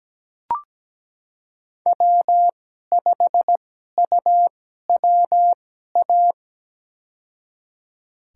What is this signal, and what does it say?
Morse code, heard as a keyed tone, about 17 wpm, spaced as for 12 wpm, 700 Hz. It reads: W5UWA
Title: USA Call Signs - 2x Character Spacing - 17wpm